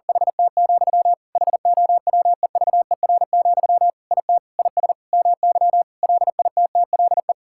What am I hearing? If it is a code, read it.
Morse code, heard as a keyed tone, about 40 wpm, 700 Hz. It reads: HT, HOWEVER, IT IS MY LITTLE
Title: Sentences from Top 1600 Words 40wpm